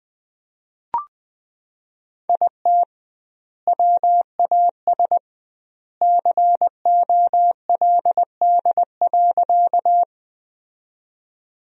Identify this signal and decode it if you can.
Morse code, heard as a keyed tone, about 20 wpm, 700 Hz. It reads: IT WAS COLD.
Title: 2x Word-Spacing - Sentences from Top 400 Words - 20wpm